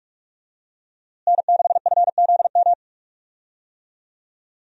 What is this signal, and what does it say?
Morse code, heard as a keyed tone, about 46 wpm, 700 Hz. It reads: N6FZM
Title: USA Call Signs - 5 Characters - Repeated 3x - 45wpm